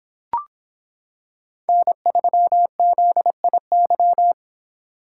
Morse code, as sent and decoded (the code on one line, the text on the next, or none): -. ...-- --.. .. -.--
N3ZIY